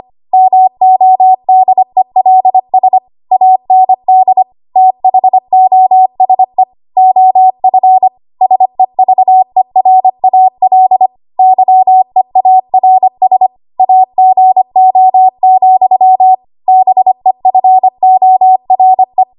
Morse code, heard as a keyed tone, about 25 wpm, 750 Hz. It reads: MODELS AND THOSE OF SEVERAL YEARS AGO, BEFORE
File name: Archive / 25 WPM / 25WPM